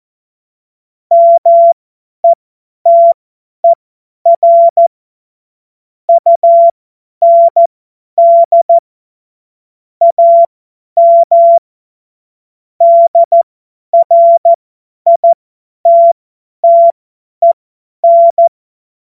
Morse code, heard as a keyed tone, about 14 wpm, spaced as for 10 wpm, 680 Hz. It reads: METER UND AM DRITTEN
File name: Morsekurs / Witze / Witz 30. Dezember 2025 50 F70